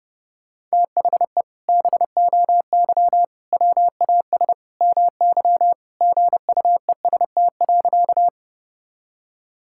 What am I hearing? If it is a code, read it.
Morse code, heard as a keyed tone, about 30 wpm, 700 Hz. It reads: THE BOY WAS MY GUEST.
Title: Sentences from Top 1600 Words 30wpm